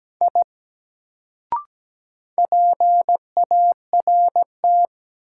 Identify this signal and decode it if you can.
Morse code, heard as a keyed tone, about 17 wpm, 700 Hz. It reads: I PART
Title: Sentences from Top 200 Words - New Word Drill - 17wpm